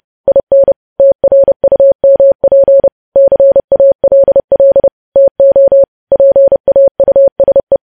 Morse code, heard as a keyed tone, about 30 wpm, 550 Hz. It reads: IN TRUMP CALL TO PAUSE
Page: news headlines in morse code at 30 wpm